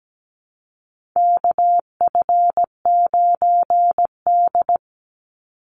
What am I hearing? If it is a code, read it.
Morse code, heard as a keyed tone, about 17 wpm, 700 Hz. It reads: KF9D
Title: Call Signs Repeated 2x - 17wpm